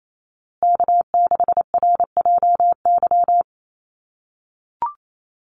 Morse code, heard as a keyed tone, about 28 wpm, 700 Hz. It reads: K6RJY
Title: USA Call Signs Repeated 2x 28wpm